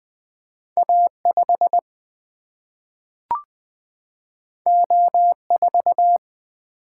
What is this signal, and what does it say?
Morse code, heard as a keyed tone, about 20 wpm, 700 Hz. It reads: A5 O4
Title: Single-Letter Single-Number 20wpm